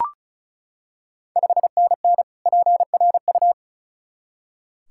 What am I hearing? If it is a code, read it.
Morse code, heard as a keyed tone, about 36 wpm, 700 Hz. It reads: 5NN PRU
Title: Ontario QSO Party Exchange 35wpm